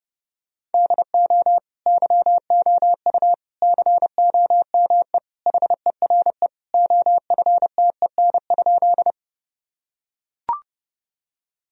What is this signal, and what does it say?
Morse code, heard as a keyed tone, about 32 wpm, 700 Hz. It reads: DO YOU COME HERE OFTEN?
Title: Sentences from Top 1000 Words - Review Part 1 - 30wpm